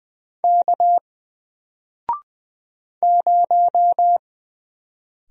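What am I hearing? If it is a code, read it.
Morse code, heard as a keyed tone, about 20 wpm, 700 Hz. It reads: K 0
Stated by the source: Single letter and number Morse drill at 20 wpm, from N4LCD.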